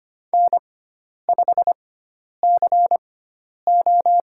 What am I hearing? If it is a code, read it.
Morse code, heard as a keyed tone, about 25 wpm, spaced as for 10 wpm, 700 Hz. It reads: N5CO
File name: Lesson 16g - Letter c - Call Signs 25wpm